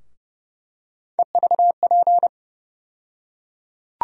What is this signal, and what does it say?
Morse code, heard as a keyed tone, about 31 wpm, 700 Hz. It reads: EVP
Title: USA Call Sign Postfixes - 3 Letters 30wpm